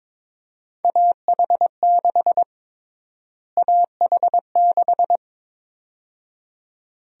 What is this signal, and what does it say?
Morse code, heard as a keyed tone, about 22 wpm, 700 Hz. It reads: AH6 AH6
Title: USA Call Sign Prefixes - Repeated 2x - 22wpm